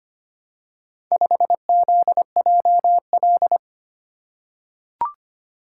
Morse code, heard as a keyed tone, about 25 wpm, 700 Hz. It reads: 5ZJL
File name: practice / 25wpm / Four Letter-Number Combinations - 25wpm